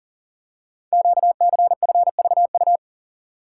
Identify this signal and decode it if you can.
Morse code, heard as a keyed tone, about 43 wpm, 700 Hz. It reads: QCFVU
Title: Code Groups - 5 Letters - 40wpm